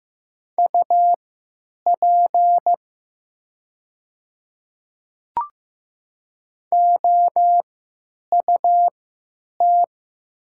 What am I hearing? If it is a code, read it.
Morse code, heard as a keyed tone, about 15 wpm, spaced as for 9 wpm, 700 Hz. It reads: UP OUT
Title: 3x Character Spacing - Top 100 Words - 15wpm